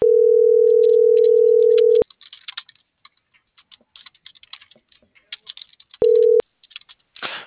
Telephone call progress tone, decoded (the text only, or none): ringback tone